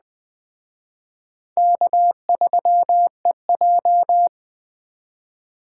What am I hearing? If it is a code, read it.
Morse code, heard as a keyed tone, about 21 wpm, 700 Hz. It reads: K3EJ
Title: Speed Racing - USA 1x2 Call Signs 17wpm